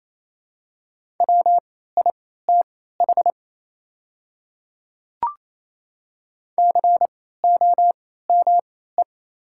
Morse code, heard as a keyed tone, about 27 wpm, spaced as for 16 wpm, 700 Hz. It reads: WITH COME
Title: 3x Character Spacing - Top 100 Words - 28wpm